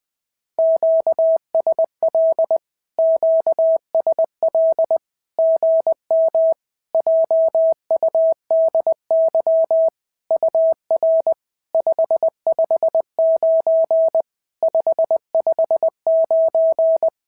Morse code, heard as a keyed tone, about 20 wpm, 650 Hz. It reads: QSL QSL GM JUDY UR 559 559
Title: SOTA QSO Practice 20wpm